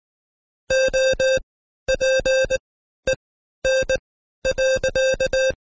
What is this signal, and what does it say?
Morse code, heard as a keyed tone, about 20 wpm, spaced as for 12 wpm, 530 Hz. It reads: OPEN.
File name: Code Quick / AudioDownld2 / Seg132-Stepper5 11WPM